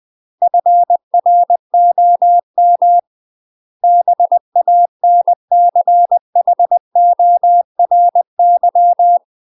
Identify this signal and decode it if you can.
Morse code, heard as a keyed tone, about 20 wpm, 700 Hz. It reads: FROM BANCHORY
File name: UK MASTER QSOs TRAINING / 20WPM / UK MASTER QSOs 0060